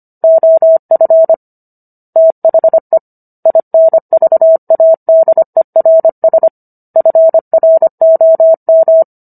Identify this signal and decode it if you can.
Morse code, heard as a keyed tone, about 25 wpm, 650 Hz. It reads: OF THE INVADERS FROM